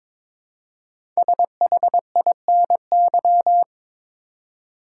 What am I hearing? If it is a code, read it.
Morse code, heard as a keyed tone, about 22 wpm, 700 Hz. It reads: SHINY
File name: 5 Letter Words - First 3 Letters Given - 22wpm